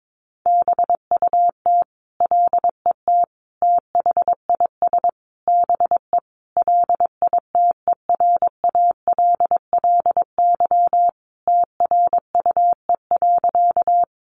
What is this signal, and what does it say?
Morse code, heard as a keyed tone, about 22 wpm, 700 Hz. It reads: BUT LET THIS BE LITERALLY TRUE.
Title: Sentences from Top 1400 Words 22wpm